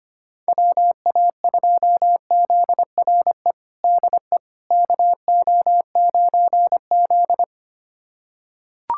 Morse code, heard as a keyed tone, about 25 wpm, 700 Hz. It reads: WA2ZRE DE KO9Z
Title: USA Callsign DE Callsign 25wpm